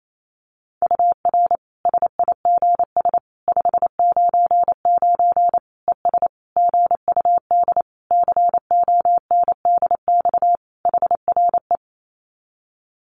Morse code, heard as a keyed tone, about 28 wpm, 700 Hz. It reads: UR SIGS 599 ES GUD CONDX HRE
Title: Common QSO Phrases 28wpm